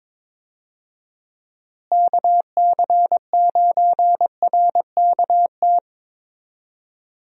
Morse code, heard as a keyed tone, about 22 wpm, 700 Hz. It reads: KC9RKT
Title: USA Call Signs - 6 Characters - Repeated 2x - 22wpm